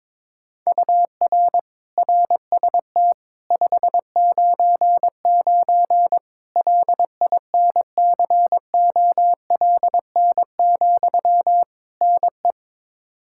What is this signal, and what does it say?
Morse code, heard as a keyed tone, about 22 wpm, 700 Hz. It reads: UR RST 599 LINCOLN, NE
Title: Practice QSO - Exchange 1 - Beginning - 22wpm